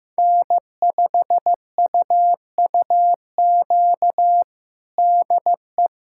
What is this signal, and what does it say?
Morse code, heard as a keyed tone, about 16 wpm, 700 Hz. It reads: N5UUQ DE
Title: USA Callsign DE Callsign 15wpm